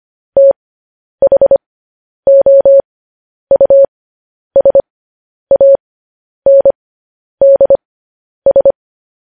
Morse code, heard as a keyed tone, about 25 wpm, spaced as for 10 wpm, 550 Hz. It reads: THOUSANDS